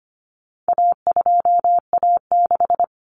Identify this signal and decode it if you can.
Morse code, heard as a keyed tone, about 25 wpm, 700 Hz. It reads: A2A6
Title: Common Cut Numbers - 4-Digit 25wpm